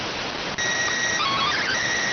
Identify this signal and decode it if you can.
SSTV (slow-scan television), PD 290.